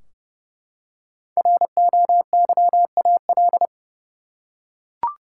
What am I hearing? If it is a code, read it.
Morse code, heard as a keyed tone, about 30 wpm, 700 Hz. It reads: ROYAL
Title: Five Letter Words - 30wpm